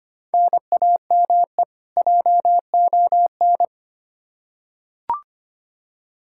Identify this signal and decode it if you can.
Morse code, heard as a keyed tone, about 25 wpm, 700 Hz. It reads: NAME JON
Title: K1USN SST QSO Practice 25wpm